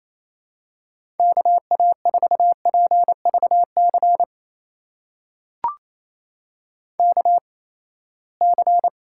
Morse code, heard as a keyed tone, about 27 wpm, spaced as for 19 wpm, 700 Hz. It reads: KA4PVC KC
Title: USA Call Signs - 8x Character Spacing - 28wpm